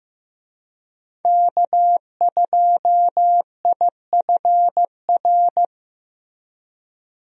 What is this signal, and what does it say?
Morse code, heard as a keyed tone, about 15 wpm, 700 Hz. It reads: K2IFR
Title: USA Call Signs - 5 Characters - Repeated 3x - 15wpm